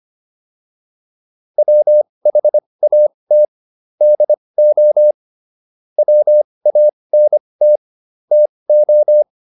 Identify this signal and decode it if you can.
Morse code, heard as a keyed tone, about 25 wpm, spaced as for 20 wpm, 600 Hz. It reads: WHAT DO WANT TO